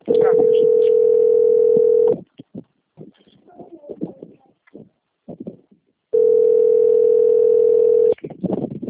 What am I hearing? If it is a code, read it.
Ringback tone (North America), 440 + 480 Hz.